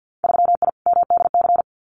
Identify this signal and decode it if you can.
Morse code, heard as a keyed tone, about 50 wpm, 700 Hz. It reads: 3I MDC
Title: Winter Field Day Exchanges Encore 50wpm